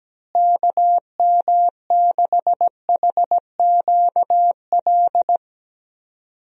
Morse code, heard as a keyed tone, about 17 wpm, 700 Hz. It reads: KM6HQL
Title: USA Call Signs - 6 Characters - Repeated 3x - 17wpm